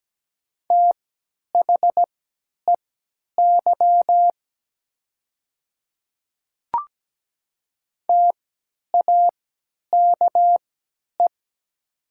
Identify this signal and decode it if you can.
Morse code, heard as a keyed tone, about 17 wpm, spaced as for 10 wpm, 700 Hz. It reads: THEY TAKE
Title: 3x Character Spacing - Top 200 Words - 17wpm